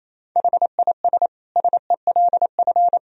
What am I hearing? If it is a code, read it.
Morse code, heard as a keyed tone, about 28 wpm, 700 Hz. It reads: HIS SELF
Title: Sentences from Top 1000 Words - Review Part 1 - 28wpm